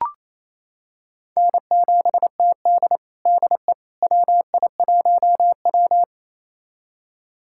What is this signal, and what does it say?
Morse code, heard as a keyed tone, about 28 wpm, 700 Hz. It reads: N7TD DE WI1W
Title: USA Callsign DE Callsign - 4 Characters - 28wpm